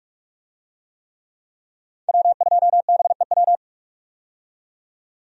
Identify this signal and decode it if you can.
Morse code, heard as a keyed tone, about 46 wpm, 700 Hz. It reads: W2BEW
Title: USA Call Signs - 5 Characters - 45wpm